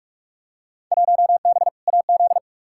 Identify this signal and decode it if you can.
Morse code, heard as a keyed tone, about 46 wpm, 700 Hz. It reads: 1B AZ
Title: Field Day Exchanges Encore - 45wpm